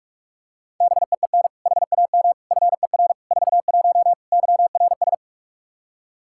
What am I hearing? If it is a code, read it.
Morse code, heard as a keyed tone, about 46 wpm, 700 Hz. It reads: BEEN HAM FER 41 YRS
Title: Practice QSO - Exchange 3 - More Info - 45wpm